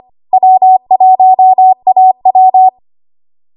Morse code, heard as a keyed tone, about 25 wpm, 750 Hz. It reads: W1AW
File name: Morse / 25WPM